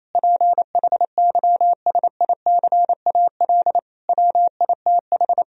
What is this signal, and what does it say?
Morse code, heard as a keyed tone, about 28 wpm, 700 Hz. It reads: PHYSICAL WITH
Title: Sentences from Top 1000 Words - Review Part 4 - 28wpm